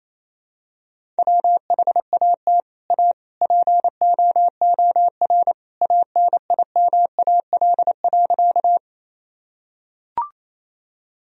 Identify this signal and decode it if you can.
Morse code, heard as a keyed tone, about 28 wpm, 700 Hz. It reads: WHAT A POOR ANIMAL.
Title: Sentences from Top 1000 Words - Review Part 1 - 28wpm